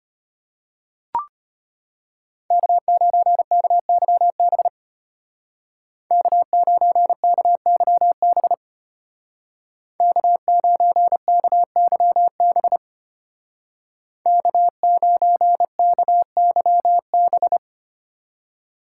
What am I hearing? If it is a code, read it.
Morse code, heard as a keyed tone, about 31 wpm, 700 Hz. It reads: K9KYB K9KYB K9KYB K9KYB
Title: Speed Racing - USA 1x3 Call Signs 25wpm